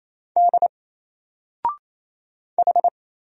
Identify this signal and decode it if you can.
Morse code, heard as a keyed tone, about 27 wpm, 700 Hz. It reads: D H
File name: Single-Letter-Number-28wpm